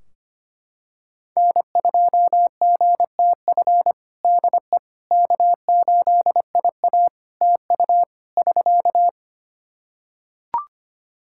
Morse code, heard as a keyed tone, about 25 wpm, 700 Hz. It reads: N2GTF DE K8IA TU <SK>
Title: Practice QSO - Exchange 4 - Ending - 25wpm